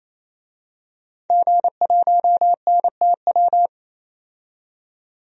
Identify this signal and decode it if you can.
Morse code, heard as a keyed tone, about 28 wpm, 700 Hz. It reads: G1NTW
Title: DXCC-100 Call Signs - 5 Characters - 28wpm